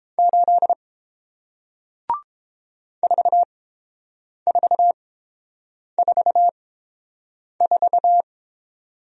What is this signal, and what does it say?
Morse code, heard as a keyed tone, about 31 wpm, spaced as for 7 wpm, 700 Hz. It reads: 8 4444